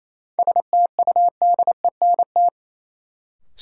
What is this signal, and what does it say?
Morse code, heard as a keyed tone, about 28 wpm, 700 Hz. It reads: STUDENT
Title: Sentences from Top 1000 Words - New Word Drill - 28wpm